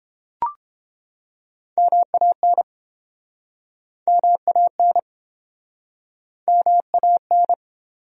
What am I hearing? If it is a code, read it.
Morse code, heard as a keyed tone, about 28 wpm, 700 Hz. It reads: MAN MAN MAN